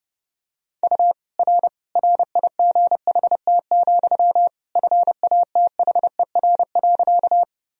Morse code, heard as a keyed tone, about 30 wpm, 700 Hz. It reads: U R RIGHT, FATHER.
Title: Sentences from Top 300 Words and Abbreviations 30wpm